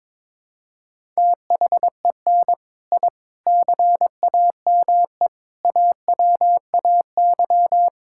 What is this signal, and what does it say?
Morse code, heard as a keyed tone, about 22 wpm, 700 Hz. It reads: THEN I CAME AWAY
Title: Sentences from Top 1000 Words - Review Part 6 - 22wpm